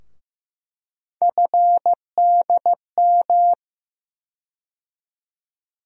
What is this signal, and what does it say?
Morse code, heard as a keyed tone, about 15 wpm, 700 Hz. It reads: FDM
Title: USA Call Sign Postfixes - 3 Letters 15wpm